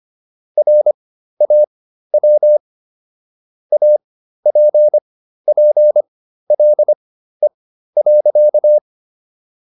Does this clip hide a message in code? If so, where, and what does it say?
Yes: Morse code from 0.6 to 8.8 s, heard as a keyed tone, about 25 wpm, spaced as for 13 wpm, 600 Hz. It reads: RAW APPLE.